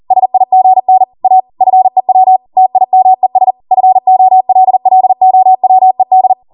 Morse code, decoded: SIGN A FEW TIMES FOLLOWED